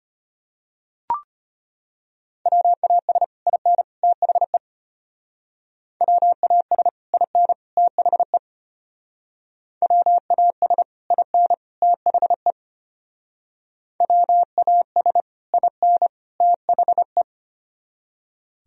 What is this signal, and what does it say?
Morse code, heard as a keyed tone, about 30 wpm, 700 Hz. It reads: WAS IN THE WAS IN THE WAS IN THE WAS IN THE
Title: Speed-Racing - Sets of 3 Words 25wpm